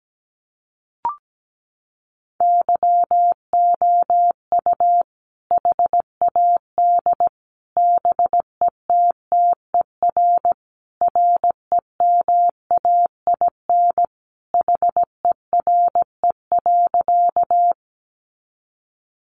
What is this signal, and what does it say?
Morse code, heard as a keyed tone, about 17 wpm, 700 Hz. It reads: YOU HAD BETTER REMAIN HERE.
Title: Sentences from Top 1200 Words 17wpm